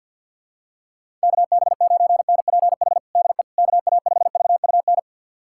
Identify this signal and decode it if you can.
Morse code, heard as a keyed tone, about 51 wpm, 700 Hz. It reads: KB9NPH DE KA5VUN
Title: USA Callsign DE Callsign 50wpm